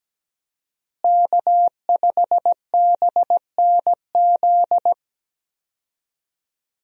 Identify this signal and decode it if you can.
Morse code, heard as a keyed tone, about 17 wpm, 700 Hz. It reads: K5BNZ
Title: USA Call Signs - 5 Characters - Repeated 2x - 17wpm